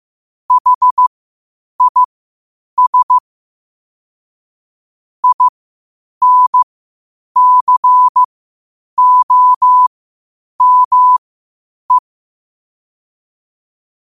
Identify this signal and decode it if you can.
Morse code, heard as a keyed tone, about 15 wpm, spaced as for 8 wpm, 1000 Hz. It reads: HIS INCOME
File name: qotdfiles / 8wpm